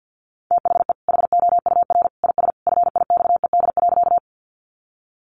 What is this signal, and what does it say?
Morse code, heard as a keyed tone, about 50 wpm, 700 Hz. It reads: THE HOUR IS FIXED.
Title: Sentences from Top 1100 Words 50wpm